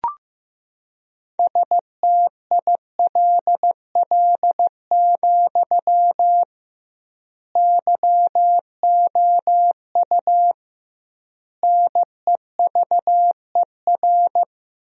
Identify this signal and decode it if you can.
Morse code, heard as a keyed tone, about 15 wpm, 700 Hz. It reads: STILL, YOU NEVER
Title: 2x Word-Spacing - Sentences from Top 300 Words - 15wpm